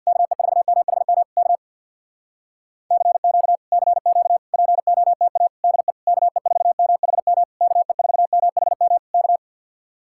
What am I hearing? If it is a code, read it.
Morse code, heard as a keyed tone, about 64 wpm, 700 Hz. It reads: KE4MHM K CQ CQ POTA DE KE4MHM KE4MHM K